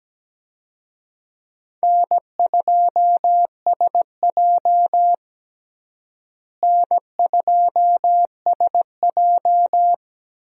Morse code, heard as a keyed tone, about 17 wpm, 700 Hz. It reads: N2SJ N2SJ
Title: USA Call Signs - 4 Characters - Repeated 2x - 17wpm